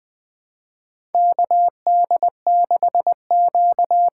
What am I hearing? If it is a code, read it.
Morse code, heard as a keyed tone, about 20 wpm, 700 Hz. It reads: K D 6 Q
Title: USA Call Signs - 6x Character Spacing - 20wpm